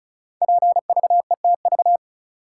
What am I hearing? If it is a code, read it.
Morse code, heard as a keyed tone, about 36 wpm, 700 Hz. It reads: PVETV
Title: Code Groups - 5 Letters - 35wpm